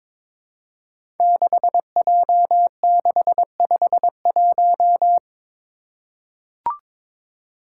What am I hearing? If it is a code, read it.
Morse code, heard as a keyed tone, about 22 wpm, 700 Hz. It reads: 6J651
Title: Five Challenging Letter-Number Combinations 22wpm